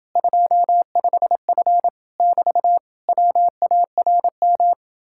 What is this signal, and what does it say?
Morse code, heard as a keyed tone, about 31 wpm, 700 Hz. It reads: 25F <BT> WARM